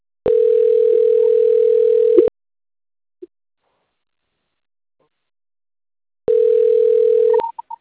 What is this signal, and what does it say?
Ringback tone (North America), 440 + 480 Hz.